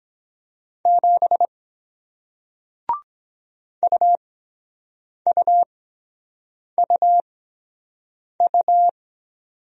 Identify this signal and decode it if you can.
Morse code, heard as a keyed tone, about 23 wpm, spaced as for 7 wpm, 700 Hz. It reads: 7 UUUU